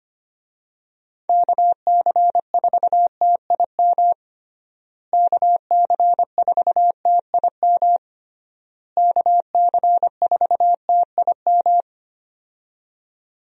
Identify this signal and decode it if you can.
Morse code, heard as a keyed tone, about 25 wpm, 700 Hz. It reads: KC4TIM KC4TIM KC4TIM
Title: USA Call Signs - 6 Characters - Repeated 3x - 25wpm